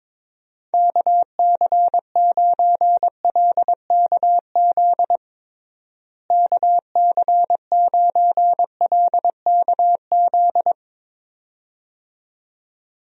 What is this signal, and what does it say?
Morse code, heard as a keyed tone, about 21 wpm, 700 Hz. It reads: KC9LKZ KC9LKZ
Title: USA Call Signs - 6 Characters - Repeated 2x - 22wpm